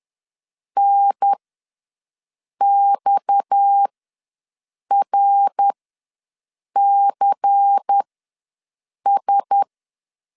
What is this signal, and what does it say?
Morse code, heard as a keyed tone, about 10 wpm, spaced as for 6 wpm, 800 Hz. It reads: NXRCS